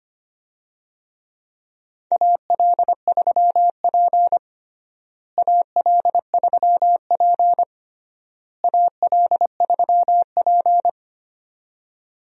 Morse code, heard as a keyed tone, about 25 wpm, 700 Hz. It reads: AL3P AL3P AL3P
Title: USA Call Signs - 4 Characters - Repeated 3x - 25wpm